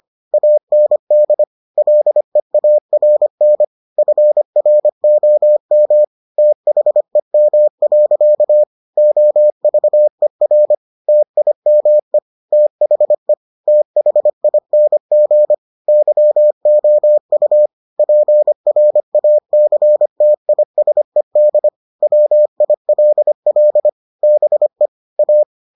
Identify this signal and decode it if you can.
Morse code, heard as a keyed tone, about 25 wpm, 600 Hz. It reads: AND LEARN FROM THEM. OVER TIME THE THING YOU PRACTISED WILL BE A